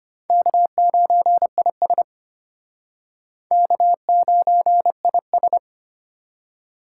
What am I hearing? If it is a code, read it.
Morse code, heard as a keyed tone, about 34 wpm, 700 Hz. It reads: K9IS K9IS